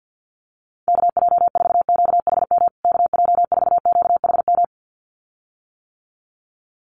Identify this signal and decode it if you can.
Morse code, heard as a keyed tone, about 50 wpm, 700 Hz. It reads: KJ4QHM KJ4QHM
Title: USA Call Signs Repeated 2x 50wpm